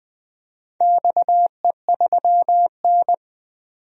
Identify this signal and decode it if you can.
Morse code, heard as a keyed tone, about 20 wpm, 700 Hz. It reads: XE3N
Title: DXCC-100 Call Signs - 4 Characters - 20wpm